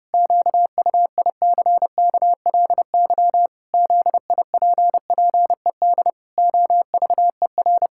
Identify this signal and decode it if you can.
Morse code, heard as a keyed tone, about 32 wpm, 700 Hz. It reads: QUICKLY ZIPPED OVER
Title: Pangram Challenge v2 - 30wpm